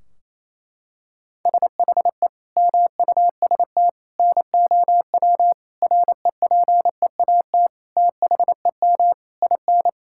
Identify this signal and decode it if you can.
Morse code, heard as a keyed tone, about 28 wpm, 700 Hz. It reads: SHE MUST NOW REPEAT THEM IN
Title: Sentences from Top 1600 Words 28wpm